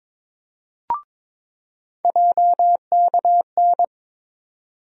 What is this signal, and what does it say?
Morse code, heard as a keyed tone, about 22 wpm, 700 Hz. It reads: JKN
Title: USA Call Sign Postfixes - 3 Letters 22wpm